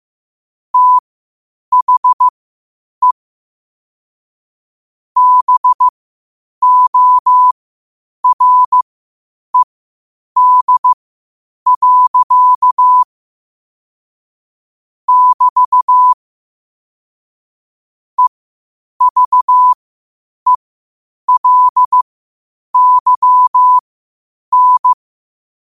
Morse code, heard as a keyed tone, about 15 wpm, spaced as for 8 wpm, 1000 Hz. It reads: THE BORED. <BT> EVELYN
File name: qotdfiles / 8wpm